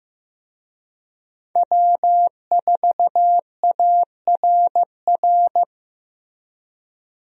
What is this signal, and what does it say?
Morse code, heard as a keyed tone, about 16 wpm, 700 Hz. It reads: W4ARR
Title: USA Call Signs - 5 Characters - Repeated 3x - 15wpm